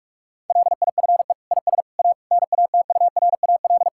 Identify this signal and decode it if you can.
Morse code, heard as a keyed tone, about 46 wpm, 700 Hz. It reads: LIFE IS A NATURAL
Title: Sentences from Top 1000 Words - Review Part 6 - 45wpm